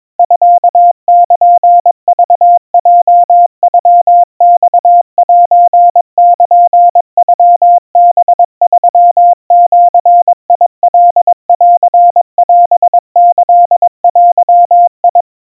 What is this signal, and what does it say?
Morse code (Japanese Wabun code), heard as a keyed tone, about 22 wpm, 700 Hz. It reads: ミルクヲノマセルノハ3ジカンオキデ
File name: kamyu 80 cpm 700 Hz Crispy